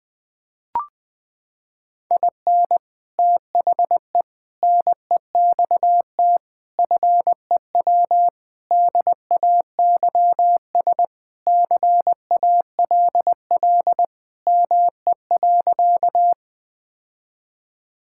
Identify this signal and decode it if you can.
Morse code, heard as a keyed tone, about 21 wpm, 700 Hz. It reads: IN THE NEXT FEW DAYS CALL ME.
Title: Sentences with 3-Word Phrases - 20wpm